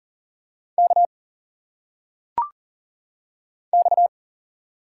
Morse code, heard as a keyed tone, about 40 wpm, 700 Hz. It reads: K X
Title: Single Letters - Phonetic Pronunciation 40wpm